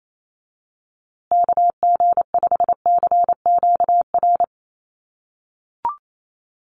Morse code, heard as a keyed tone, about 28 wpm, 700 Hz. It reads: KG5CQR
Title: USA Call Signs Repeated 3x 28wpm